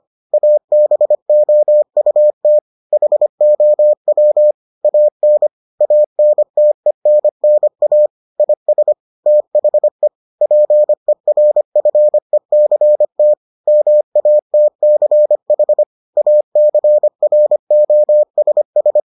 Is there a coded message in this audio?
Yes: Morse code from 0.3 to 19.0 s, heard as a keyed tone, about 25 wpm, 600 Hz. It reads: ABOUT HOW AN ANTENNA IS THE PERFECT MATCH ACROSS